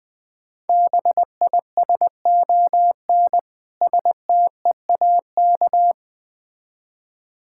Morse code, heard as a keyed tone, about 21 wpm, 700 Hz. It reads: BISON STEAK
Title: Difficult Words with Hints - 20wpm